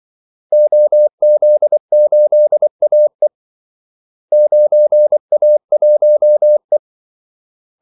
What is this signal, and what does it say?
Morse code, heard as a keyed tone, about 24 wpm, 600 Hz. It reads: OZ8AE 9A1E
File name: Callsigns-24wpm